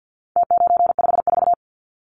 Morse code, heard as a keyed tone, about 50 wpm, 700 Hz. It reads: T954
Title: Numbers - Contest Style - 4-Digit 50wpm